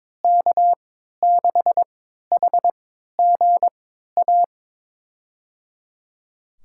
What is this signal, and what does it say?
Morse code, heard as a keyed tone, about 22 wpm, spaced as for 13 wpm, 700 Hz. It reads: K6HGA
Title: USA Call Signs - 3x Character Spacing - 22wpm